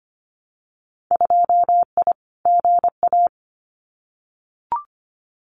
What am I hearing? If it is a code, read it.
Morse code, heard as a keyed tone, about 25 wpm, 700 Hz. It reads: 2I GA
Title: Winter Field Day Exchanges 25wpm